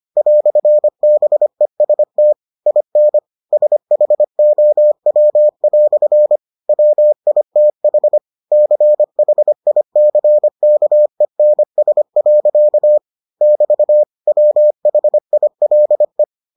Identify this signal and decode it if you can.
Morse code, heard as a keyed tone, about 25 wpm, 600 Hz. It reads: "BEST IN SHOW" WITH CHICKENS. <BT> WHILE